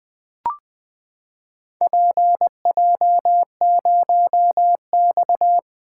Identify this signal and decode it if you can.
Morse code, heard as a keyed tone, about 20 wpm, 700 Hz. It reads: PJ0X
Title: Four Letter-Number Combinations - 20wpm